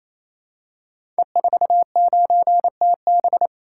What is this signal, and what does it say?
Morse code, heard as a keyed tone, about 27 wpm, 700 Hz. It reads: E49TB
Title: Code Groups - 5 Letter:Numbers - 28wpm